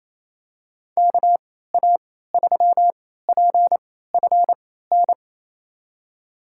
Morse code, heard as a keyed tone, about 28 wpm, spaced as for 16 wpm, 700 Hz. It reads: KA3PFN
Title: USA Call Signs - 6 Characters - 3x Character Spacing - 28wpm